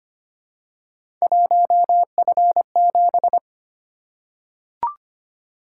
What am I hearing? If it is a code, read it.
Morse code, heard as a keyed tone, about 25 wpm, 700 Hz. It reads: 1F7
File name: Three Letter-Number Combinations - 25wpm